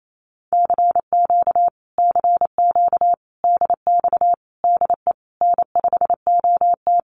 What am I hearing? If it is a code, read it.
Morse code, heard as a keyed tone, about 28 wpm, 700 Hz. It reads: CQ CQ DX DE N5OT
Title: Common QSO Phrases Encore 28wpm